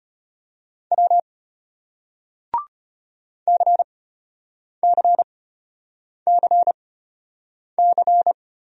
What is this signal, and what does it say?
Morse code, heard as a keyed tone, about 34 wpm, spaced as for 8 wpm, 700 Hz. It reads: W CCCC